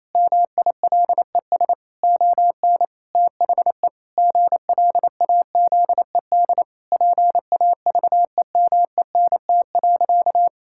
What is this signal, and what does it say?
Morse code, heard as a keyed tone, about 28 wpm, 700 Hz. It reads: MILES ON THE GLAZED PAVEMENT.
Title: Pangram Challenge v2 - 28wpm